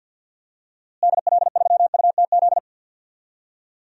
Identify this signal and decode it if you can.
Morse code, heard as a keyed tone, about 51 wpm, 700 Hz. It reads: DL3UTZ
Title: DXCC-100 Call Signs - 6 Characters - Repeated 3x - 50wpm